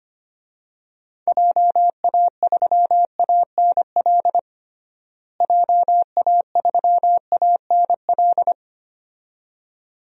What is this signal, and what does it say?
Morse code, heard as a keyed tone, about 25 wpm, 700 Hz. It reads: JA3ANL JA3ANL
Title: DXCC-100 Call Signs - 6 Characters - Repeated 2x - 25wpm